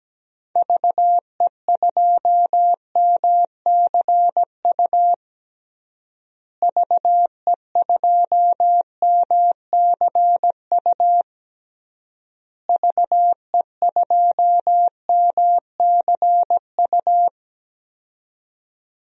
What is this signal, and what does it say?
Morse code, heard as a keyed tone, about 17 wpm, 700 Hz. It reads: VE2MCU VE2MCU VE2MCU
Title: DXCC-100 Call Signs - 6 Characters - Repeated 3x - 17wpm